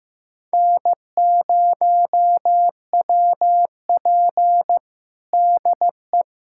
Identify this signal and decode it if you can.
Morse code, heard as a keyed tone, about 15 wpm, 700 Hz. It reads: N0WP DE
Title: USA Callsign DE Callsign - 4 Characters - 15wpm